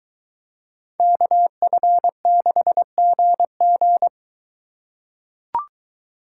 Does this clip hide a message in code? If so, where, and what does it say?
Yes: Morse code from 1.0 to 4.1 s, heard as a keyed tone, about 22 wpm, 700 Hz. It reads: KF6GG